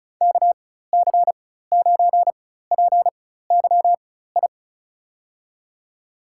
Morse code, heard as a keyed tone, about 34 wpm, spaced as for 16 wpm, 700 Hz. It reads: KC9PYI
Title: USA Call Signs - 4x Character Spacing - 35wpm